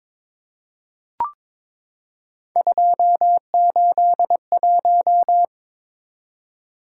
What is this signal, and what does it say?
Morse code, heard as a keyed tone, about 22 wpm, 700 Hz. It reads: 281
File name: practice / 22wpm / Three Letter-Number Combinations - 22wpm